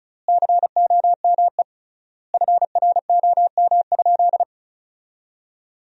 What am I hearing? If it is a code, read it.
Morse code, heard as a keyed tone, about 36 wpm, 700 Hz. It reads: COME FROM?
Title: 3x Word-Spacing - Sentences from Top 300 Words - 35wpm